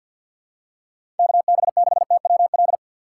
Morse code, heard as a keyed tone, about 50 wpm, 700 Hz. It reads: KB6TWL
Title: USA Call Signs - 2x Character Spacing - 50wpm